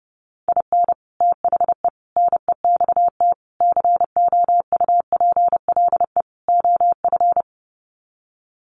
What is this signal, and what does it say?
Morse code, heard as a keyed tone, about 30 wpm, 700 Hz. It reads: IN THE NEXT COUPLE OF
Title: Sets of 5 Words Encore2 30wpm